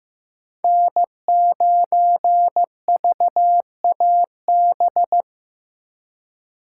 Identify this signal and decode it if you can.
Morse code, heard as a keyed tone, about 16 wpm, 700 Hz. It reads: N9VAB
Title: USA Call Signs - 5 Characters - 4x Character Spacing - 15wpm